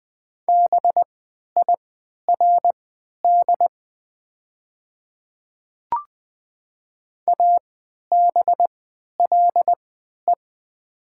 Morse code, heard as a keyed tone, about 20 wpm, spaced as for 11 wpm, 700 Hz. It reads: BIRD ABLE
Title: 3x Character Spacing - Top 500 Words - 20wpm